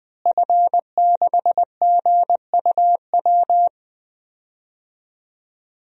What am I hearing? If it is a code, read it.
Morse code, heard as a keyed tone, about 20 wpm, 700 Hz. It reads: F6GUW
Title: DXCC-100 Call Signs - 5 Characters - Repeated 2x - 20wpm